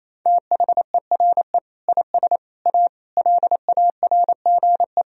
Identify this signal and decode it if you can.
Morse code, heard as a keyed tone, about 28 wpm, 700 Hz. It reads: THERE IS A LARGE
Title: Sentences from Top 1000 Words - Review Part 4 - 28wpm